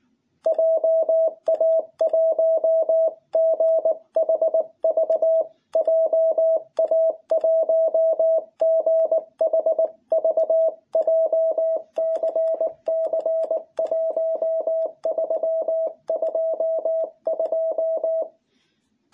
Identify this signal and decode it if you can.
Morse code, heard as a keyed tone, about 19 wpm, 660 Hz. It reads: JA1GHV JA1GHV JCC 1322
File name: jh8cbh / 2021cw